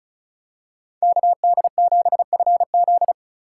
Morse code, heard as a keyed tone, about 36 wpm, 700 Hz. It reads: KD7FZ
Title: USA Call Signs - 5 Characters - 3x Character Spacing - 35wpm